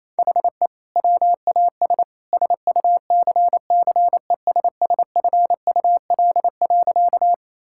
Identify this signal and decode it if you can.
Morse code, heard as a keyed tone, about 28 wpm, 700 Hz. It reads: HE WAS SUCCESSFUL.
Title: Sentences from Top 1000 Words - Review Part 6 - 28wpm